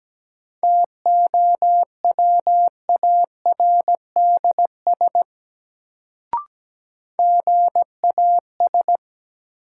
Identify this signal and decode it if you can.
Morse code, heard as a keyed tone, about 17 wpm, 700 Hz. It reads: TOWARDS GAS
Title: Sentences from Top 1100 Words - New Word Drill Alt - 17wpm